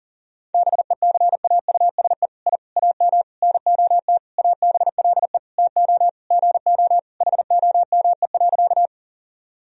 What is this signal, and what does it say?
Morse code, heard as a keyed tone, about 40 wpm, 700 Hz. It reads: BECAUSE I AM NOT ABLE TO GO HOME.
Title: Sentences from Top 1000 Words - Review Part 3 - 40wpm